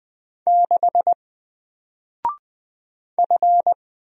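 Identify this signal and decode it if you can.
Morse code, heard as a keyed tone, about 20 wpm, 700 Hz. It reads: 6 F
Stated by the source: Single letter and number Morse drill at 20 wpm, from N4LCD.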